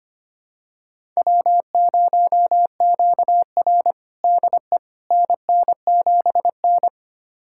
Morse code, heard as a keyed tone, about 25 wpm, 700 Hz. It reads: W0QR DE NN7N
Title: USA Callsign DE Callsign - 4 Characters - 25wpm